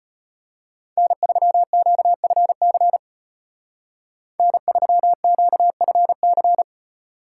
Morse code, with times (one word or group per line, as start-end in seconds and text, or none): 1.0-3.0 N3QFC
4.4-6.6 N3QFC